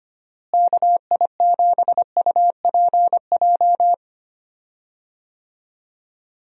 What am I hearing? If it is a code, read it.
Morse code, heard as a keyed tone, about 25 wpm, 700 Hz. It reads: KI7UPJ
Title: USA Call Signs - 6 Characters - Repeated 2x - 25wpm